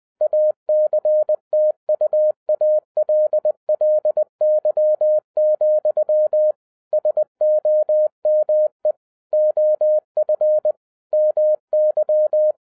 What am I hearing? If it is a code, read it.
Morse code, heard as a keyed tone, about 20 wpm, 610 Hz. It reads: ACTUALLY, SOME OF MY